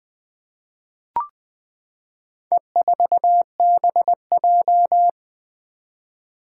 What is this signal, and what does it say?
Morse code, heard as a keyed tone, about 20 wpm, 700 Hz. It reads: E4BJ
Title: Four Challenging Letter-Number Combinations 20wpm